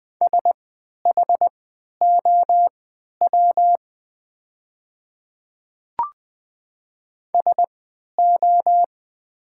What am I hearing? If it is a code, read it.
Morse code, heard as a keyed tone, about 20 wpm, spaced as for 11 wpm, 690 Hz. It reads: SHOW SO
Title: 3x Character Spacing - Top 200 Words - 20wpm